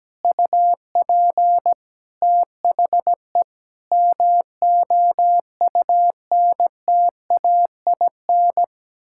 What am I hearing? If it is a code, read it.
Morse code, heard as a keyed tone, about 17 wpm, 700 Hz. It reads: UP THE MOUNTAIN
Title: Sentences from Top 1000 Words - Review Part 6 - 17wpm